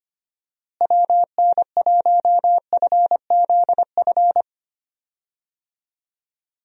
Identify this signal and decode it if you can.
Morse code, heard as a keyed tone, about 25 wpm, 700 Hz. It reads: WN1FZF
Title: USA Call Signs - 6 Characters - Repeated 2x - 25wpm